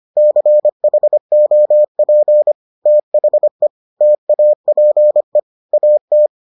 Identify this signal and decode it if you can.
Morse code, heard as a keyed tone, about 25 wpm, 600 Hz. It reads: CHOP THE TAPE AT